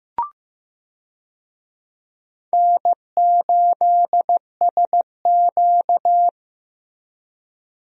Morse code, heard as a keyed tone, about 15 wpm, 700 Hz. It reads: N8SQ
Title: USA Call Signs - 4 Characters - Repeated 2x - 15wpm